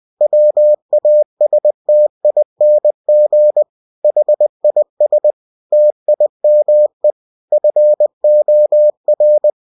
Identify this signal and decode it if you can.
Morse code, heard as a keyed tone, about 20 wpm, 600 Hz. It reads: WASTING HIS TIME FOR
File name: cw-books / winnie-the-pooh / winnie-20wpm-page-0041